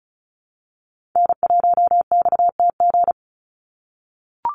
Morse code, heard as a keyed tone, about 34 wpm, 700 Hz. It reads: N1XTG
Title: USA Call Signs 35wpm